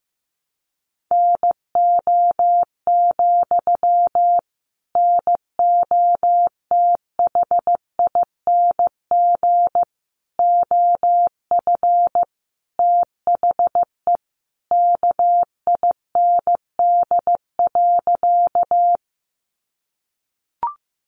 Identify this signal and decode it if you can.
Morse code, heard as a keyed tone, about 15 wpm, 700 Hz. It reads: NO, NOTHING OF THE KIND.
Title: Sentences from Top 500 Words 15wpm